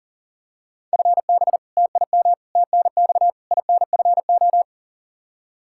Morse code, heard as a keyed tone, about 37 wpm, 700 Hz. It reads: FB TIM TNX INFO
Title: Practice QSO - Exchange 3 - More Info - 40wpm